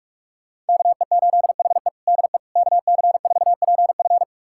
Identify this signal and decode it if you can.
Morse code, heard as a keyed tone, about 42 wpm, 700 Hz. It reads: KE8HE DE KC4PF
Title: USA Callsign DE Callsign - 5 Characters - 45wpm